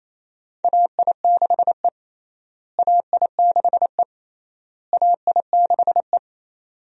Morse code, heard as a keyed tone, about 28 wpm, 700 Hz. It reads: AI6E AI6E AI6E
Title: USA Call Signs - 4 Characters - Repeated 3x - 28wpm